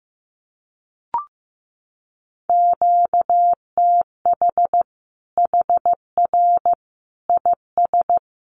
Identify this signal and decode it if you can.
Morse code, heard as a keyed tone, about 15 wpm, 700 Hz. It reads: QTH HR IS
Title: Common QSO Phrases 15wpm